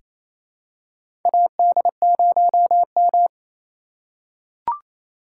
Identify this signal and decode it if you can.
Morse code, heard as a keyed tone, about 28 wpm, 700 Hz. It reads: AD0M
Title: USA Call Signs - 3x Character Spacing - 28wpm